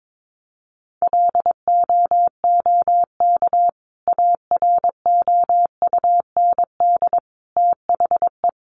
Morse code, heard as a keyed tone, about 22 wpm, 700 Hz. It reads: LOOK AROUND THE
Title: Sentences from Top 600 Words 22wpm